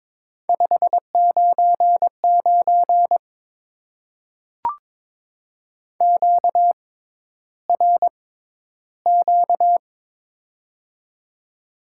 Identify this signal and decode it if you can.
Morse code, heard as a keyed tone, about 22 wpm, spaced as for 11 wpm, 700 Hz. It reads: 599 QRQ
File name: Top 90 QSO Elements - 6x Character Spacing - 22wpm